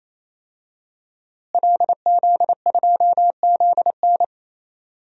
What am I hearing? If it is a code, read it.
Morse code, heard as a keyed tone, about 28 wpm, 700 Hz. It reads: LZ2ZN
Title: DXCC-100 Call Signs - 5 Characters - Repeated 3x - 28wpm